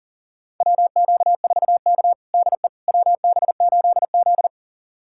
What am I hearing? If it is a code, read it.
Morse code, heard as a keyed tone, about 40 wpm, 700 Hz. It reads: WQ4K DE WB8Z
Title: USA Callsign DE Callsign - 4 Characters - 40wpm